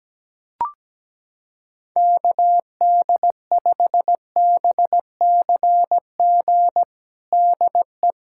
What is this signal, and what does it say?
Morse code, heard as a keyed tone, about 17 wpm, 700 Hz. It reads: KD5BCG DE
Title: USA Callsign DE Callsign 17wpm